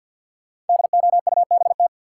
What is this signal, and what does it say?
Morse code, heard as a keyed tone, about 51 wpm, 700 Hz. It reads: DOUBT
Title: Sets of 5 Words - New Word Drill - 50wpm